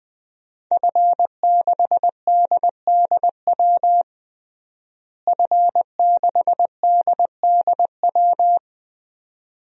Morse code, heard as a keyed tone, about 20 wpm, 700 Hz. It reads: F6DDW F6DDW
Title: DXCC-100 Call Signs - 5 Characters - Repeated 2x - 20wpm